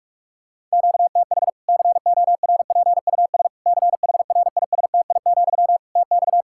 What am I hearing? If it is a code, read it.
Morse code, heard as a keyed tone, about 41 wpm, 700 Hz. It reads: QTH CORPUS CHRISTI, TX
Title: K1USN SST QSO Practice 45wpm